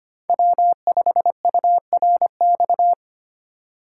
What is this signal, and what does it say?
Morse code, heard as a keyed tone, about 25 wpm, 700 Hz. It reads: W5URX
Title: USA Call Signs - 8x Character Spacing - 25wpm